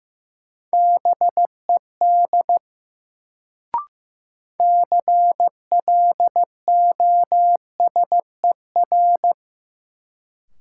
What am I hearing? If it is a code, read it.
Morse code, heard as a keyed tone, about 15 wpm, 700 Hz. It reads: BED CLOSER
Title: Sets of 5 Words Encore - New Word Drill - 15wpm